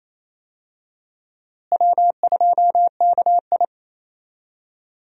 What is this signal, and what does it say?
Morse code, heard as a keyed tone, about 28 wpm, 700 Hz. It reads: W2KI
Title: USA Call Signs - 4 Characters - Repeated 2x - 28wpm